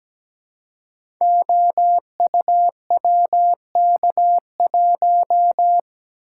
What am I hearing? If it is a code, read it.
Morse code, heard as a keyed tone, about 17 wpm, 700 Hz. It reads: OUWK1
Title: Code Groups - 5 Letter:Numbers - 17wpm